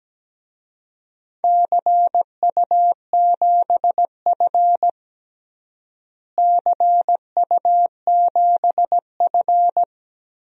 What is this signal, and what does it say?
Morse code, heard as a keyed tone, about 17 wpm, 700 Hz. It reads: CU7F CU7F
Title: DXCC-100 Call Signs - 4 Characters - Repeated 2x - 17wpm